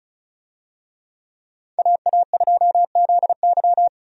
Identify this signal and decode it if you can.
Morse code, heard as a keyed tone, about 34 wpm, 700 Hz. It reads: AA2ZY
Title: USA Call Signs - 5 Characters - 35wpm